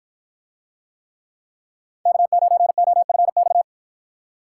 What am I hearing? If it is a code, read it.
Morse code, heard as a keyed tone, about 48 wpm, 700 Hz. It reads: K9OFX